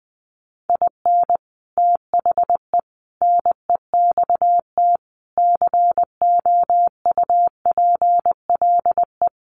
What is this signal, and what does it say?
Morse code, heard as a keyed tone, about 20 wpm, 700 Hz. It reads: IN THE NEXT COUPLE
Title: Sets of 5 Words Encore2 20wpm